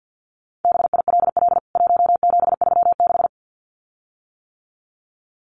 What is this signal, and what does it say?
Morse code, heard as a keyed tone, about 50 wpm, 700 Hz. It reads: BILL 1736
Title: CWT Exchange 50wpm